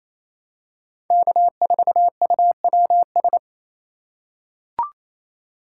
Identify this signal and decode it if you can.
Morse code, heard as a keyed tone, about 28 wpm, 700 Hz. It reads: K4UWS
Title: USA Call Signs - 5 Characters - 2x Character Spacing - 28wpm